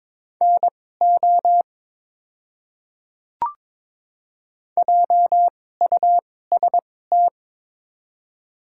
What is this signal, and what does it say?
Morse code, heard as a keyed tone, about 22 wpm, spaced as for 16 wpm, 700 Hz. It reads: NO JUST
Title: 2x Character Spacing - Top 300 Words - 22wpm